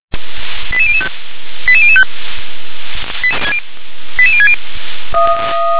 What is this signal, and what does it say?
Five-tone selective call (ZVEI), heard as quick runs of five tones, tones of about 70 ms. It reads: 88005 88005 88060